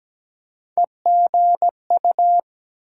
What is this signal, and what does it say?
Morse code, heard as a keyed tone, about 17 wpm, 700 Hz. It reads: EGU
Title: USA Call Sign Postfixes - 3 Letters 17wpm